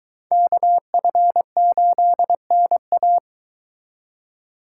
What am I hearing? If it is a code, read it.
Morse code, heard as a keyed tone, about 23 wpm, 700 Hz. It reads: KF8NA